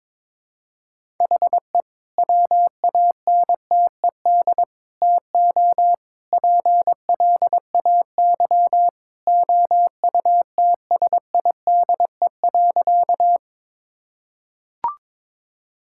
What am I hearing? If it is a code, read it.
Morse code, heard as a keyed tone, about 22 wpm, 700 Hz. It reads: HE WANTED TO PLAY OUTSIDE.
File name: practice / 22wpm / Sentences with 3-Word Phrases - 22wpm